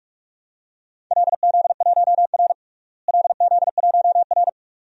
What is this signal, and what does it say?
Morse code, heard as a keyed tone, about 42 wpm, 700 Hz. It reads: LZ1R LZ1R
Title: DXCC-100 Call Signs - 4 Characters - Repeated 2x - 45wpm